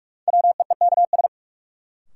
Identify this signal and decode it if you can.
Morse code, heard as a keyed tone, about 43 wpm, 700 Hz. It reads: WEEKS
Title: Sets of 5 Words - New Word Drill - 45wpm